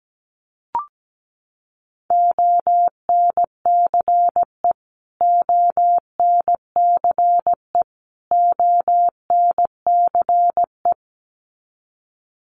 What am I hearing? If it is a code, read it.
Morse code, heard as a keyed tone, about 17 wpm, 700 Hz. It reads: ONCE ONCE ONCE
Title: Top 300 Words Repeated 17wpm